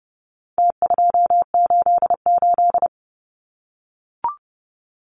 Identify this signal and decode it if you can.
Morse code, heard as a keyed tone, about 30 wpm, 700 Hz. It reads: T288
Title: Numbers - Contest Style - 4-Digit 30wpm